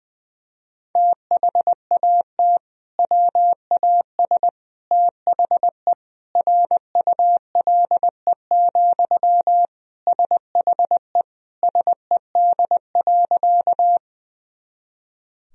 Morse code, heard as a keyed tone, about 20 wpm, 700 Hz. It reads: THAT WAS THE RULE, SHE SED.
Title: Sentences from Top 400 Words and Abbreviations 20wpm